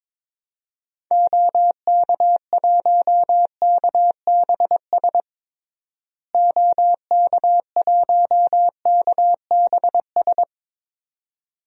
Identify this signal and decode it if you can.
Morse code, heard as a keyed tone, about 22 wpm, 700 Hz. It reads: OK1KBS OK1KBS
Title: DXCC-100 Call Signs - 6 Characters - Repeated 2x - 22wpm